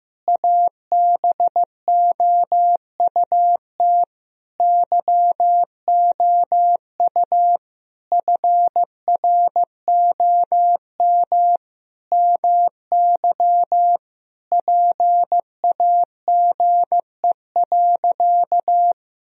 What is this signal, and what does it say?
Morse code, heard as a keyed tone, about 15 wpm, 700 Hz. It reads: ABOUT YOU FROM MY PAGE.
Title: Sentences from Top 1000 Words - Review Part 1 - 15wpm